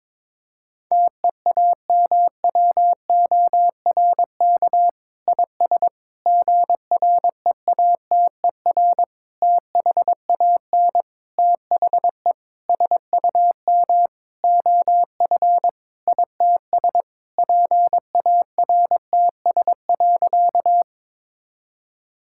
Morse code, heard as a keyed tone, about 22 wpm, 700 Hz. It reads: TEAMWORK IS GREATER THAN THE SUM OF ITS PARTS.
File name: Sentences with 7-Word Phrases - 22wpm